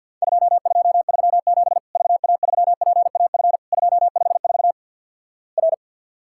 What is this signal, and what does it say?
Morse code, heard as a keyed tone, about 47 wpm, 700 Hz. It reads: 2237 VA3PAF 254 R
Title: Worked All Europe DX Contest - QTC Exchange - 50wpm